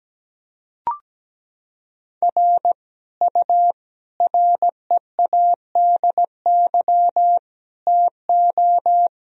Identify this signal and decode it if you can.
Morse code, heard as a keyed tone, about 17 wpm, 700 Hz. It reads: R U READY TO
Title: Sentences from Top 400 Words and Abbreviations 17wpm